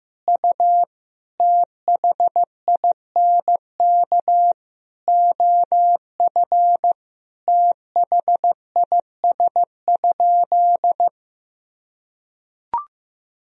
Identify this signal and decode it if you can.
Morse code, heard as a keyed tone, about 15 wpm, 700 Hz. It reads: U THINK OF THIS?
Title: Sentences from Top 200 Words and Abbreviations 15wpm